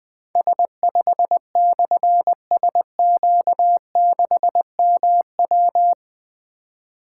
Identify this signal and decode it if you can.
Morse code, heard as a keyed tone, about 20 wpm, 700 Hz. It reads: S5/SQ6MW
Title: Difficult Call Signs 20wpm